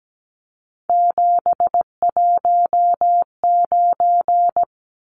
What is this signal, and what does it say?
Morse code, heard as a keyed tone, about 17 wpm, 700 Hz. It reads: 719
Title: Numbers - 3-Digit 17wpm